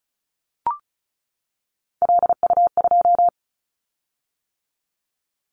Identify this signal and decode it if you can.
Morse code, heard as a keyed tone, about 35 wpm, 700 Hz. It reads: LU2
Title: Callsign Prefixes for Top 100 DXCC - 35wpm